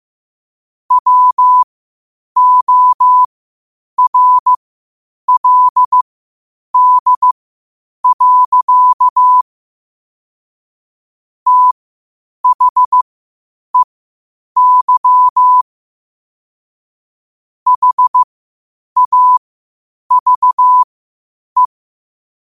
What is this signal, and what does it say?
Morse code, heard as a keyed tone, about 15 wpm, spaced as for 8 wpm, 1000 Hz. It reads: WORLD. THEY HAVE
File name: qotdfiles / 8wpm